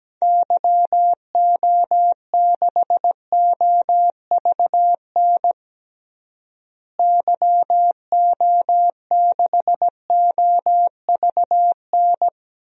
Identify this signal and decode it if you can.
Morse code, heard as a keyed tone, about 17 wpm, 700 Hz. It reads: YO6OVN YO6OVN
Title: DXCC-100 Call Signs - 6 Characters - Repeated 2x - 17wpm